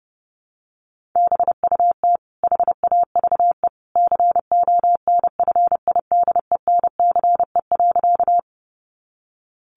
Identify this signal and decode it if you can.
Morse code, heard as a keyed tone, about 30 wpm, 700 Hz. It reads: BUT HAVE CONFIDENCE.
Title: Sentences from Top 1500 Words 30wpm